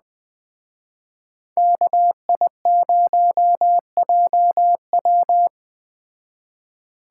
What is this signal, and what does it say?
Morse code, heard as a keyed tone, about 20 wpm, 700 Hz. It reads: KI0JW
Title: Speed Racing - USA Call Signs 17wpm